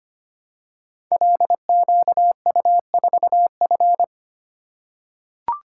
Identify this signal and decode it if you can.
Morse code, heard as a keyed tone, about 25 wpm, 700 Hz. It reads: LQU4F
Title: Code Groups - 5 Letter:Numbers - 25wpm